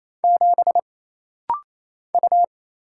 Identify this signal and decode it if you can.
Morse code, heard as a keyed tone, about 27 wpm, 700 Hz. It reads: 7 U
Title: Single Letter-Number - Rapid-Fire 28wpm